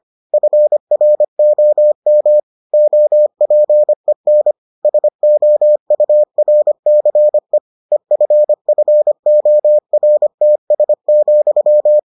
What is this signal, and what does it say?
Morse code, heard as a keyed tone, about 25 wpm, 600 Hz. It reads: FROM OPEN SOURCE EFFORTS,